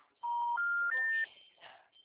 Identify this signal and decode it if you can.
Special information tone.
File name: out- -2008- - - .614